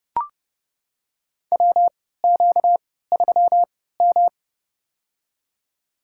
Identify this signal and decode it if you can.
Morse code, heard as a keyed tone, about 32 wpm, spaced as for 17 wpm, 700 Hz. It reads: WQ3M
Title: USA Call Signs - 4 Characters - 3x Character Spacing - 30wpm